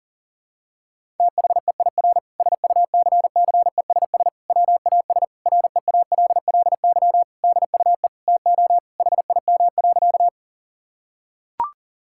Morse code, heard as a keyed tone, about 40 wpm, 700 Hz. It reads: THEIR SUCCESS WAS REALLY DUE TO HIM.
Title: Sentences from Top 1000 Words - Review Part 6 - 40wpm